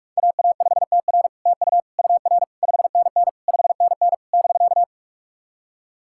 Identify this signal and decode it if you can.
Morse code, heard as a keyed tone, about 44 wpm, 700 Hz. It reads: AA5TR TU UR 5NN 5NN <BK>
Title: SOTA QSO Practice 45wpm